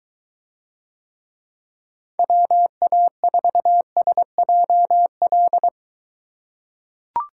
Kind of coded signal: Morse code

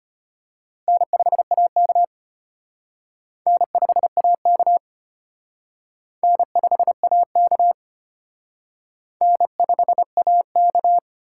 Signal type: Morse code

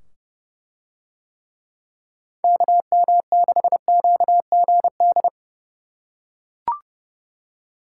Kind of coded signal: Morse code

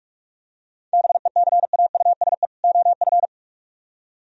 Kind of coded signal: Morse code

